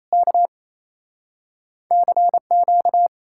Morse code, heard as a keyed tone, about 38 wpm, spaced as for 33 wpm, 700 Hz. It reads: K CQ
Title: Speed-Racing Common QSO Phrases - 28wpm